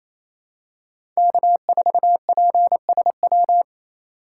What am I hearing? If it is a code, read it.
Morse code, heard as a keyed tone, about 28 wpm, 700 Hz. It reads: K4PSW
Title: USA Call Signs - 2x Character Spacing - 28wpm